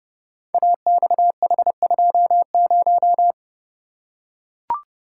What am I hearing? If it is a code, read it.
Morse code, heard as a keyed tone, about 29 wpm, 700 Hz. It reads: AXH20
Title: Code Groups - 5 Letter:Numbers - 30wpm